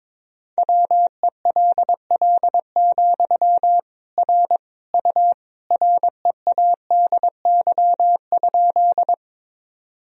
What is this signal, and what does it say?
Morse code, heard as a keyed tone, about 22 wpm, 700 Hz. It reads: WELL, R U READY?
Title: Sentences from Top 400 Words and Abbreviations 22wpm